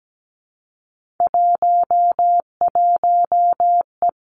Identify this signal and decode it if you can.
Morse code, heard as a keyed tone, about 17 wpm, 700 Hz. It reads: 11E
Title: Three Challenging Letter-Number Combinations 17wpm